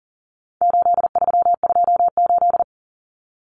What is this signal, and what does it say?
Morse code, heard as a keyed tone, about 40 wpm, 700 Hz. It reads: 8328
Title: Numbers - 4-Digit 40wpm